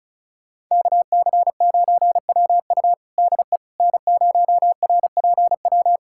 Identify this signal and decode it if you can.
Morse code, heard as a keyed tone, about 36 wpm, 700 Hz. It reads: KC9WU DE N0RPW
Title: USA Callsign DE Callsign - 5 Characters - 35wpm